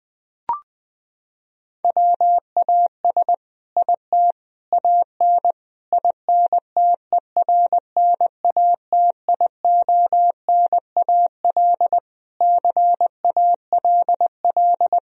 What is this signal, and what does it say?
Morse code, heard as a keyed tone, about 20 wpm, 700 Hz. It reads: WAS IT AN INTERNATIONAL CALL
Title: Sentences from Top 1000 Words - Review Part 4 - 20wpm